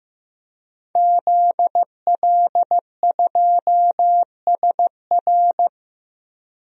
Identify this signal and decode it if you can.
Morse code, heard as a keyed tone, about 15 wpm, 700 Hz. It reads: ZL2SR
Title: DXCC-100 Call Signs - 5 Characters - Repeated 3x - 15wpm